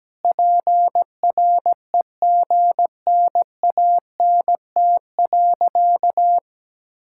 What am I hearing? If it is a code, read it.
Morse code, heard as a keyed tone, about 17 wpm, 700 Hz. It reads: PREGNANT.
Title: Sentences from Top 1000 Words - Review Part 6 - 17wpm